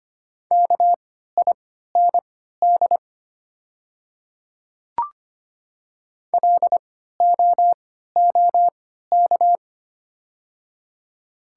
Morse code, heard as a keyed tone, about 25 wpm, spaced as for 14 wpm, 700 Hz. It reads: KIND LOOK
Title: 3x Character Spacing - Top 200 Words - 25wpm